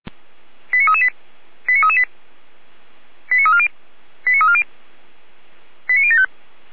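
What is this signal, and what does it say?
Five-tone selective call (ZVEI), heard as quick runs of five tones, tones of about 80 ms. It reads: 89338 89338 89340 89340 89075
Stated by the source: Received on SDR as a ZVEI five-tone selective call, address 89319, from the Feuerwehr Orlamünde fire brigade server.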